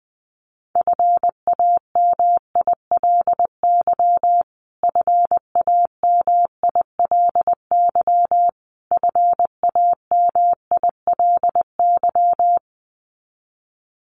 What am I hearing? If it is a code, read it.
Morse code, heard as a keyed tone, about 20 wpm, 700 Hz. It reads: FAMILY FAMILY FAMILY
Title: Top 500 Words Repeated 20wpm